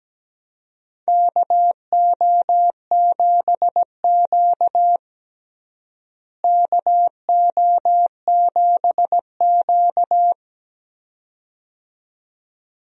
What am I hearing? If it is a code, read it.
Morse code, heard as a keyed tone, about 17 wpm, 700 Hz. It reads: KO7Q KO7Q
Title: USA Call Signs - 4 Characters - Repeated 2x - 17wpm